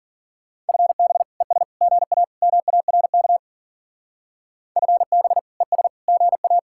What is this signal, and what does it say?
Morse code, heard as a keyed tone, about 50 wpm, 700 Hz. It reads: FB ES GA MARK FB ES GA
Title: Speed-Racing Common QSO Phrases Encore - 40wpm